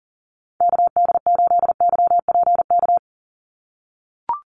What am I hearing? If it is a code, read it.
Morse code, heard as a keyed tone, about 38 wpm, 700 Hz. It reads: KD8YPK
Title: USA Call Signs 40wpm